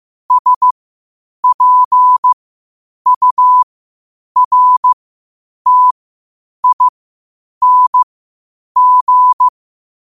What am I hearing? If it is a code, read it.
Morse code, heard as a keyed tone, about 15 wpm, spaced as for 8 wpm, 1000 Hz. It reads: SPURTING
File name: qotdfiles / 8wpm